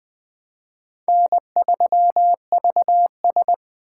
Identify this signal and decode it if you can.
Morse code, heard as a keyed tone, about 21 wpm, 700 Hz. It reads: N3VS
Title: USA Call Signs - 4 Characters - 2x Character Spacing - 20wpm